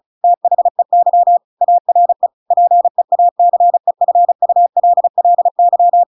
Morse code, heard as a keyed tone, about 34 wpm, 700 Hz. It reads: THEY ARE PEACEFULLY